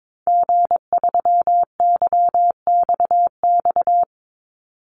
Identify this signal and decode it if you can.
Morse code, heard as a keyed tone, about 22 wpm, 700 Hz. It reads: G3YXX
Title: Call Signs - 22wpm